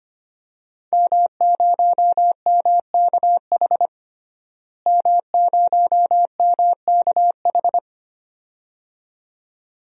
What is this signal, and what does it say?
Morse code, heard as a keyed tone, about 25 wpm, 700 Hz. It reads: M0MKH M0MKH
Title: DXCC-100 Call Signs - 5 Characters - Repeated 2x - 25wpm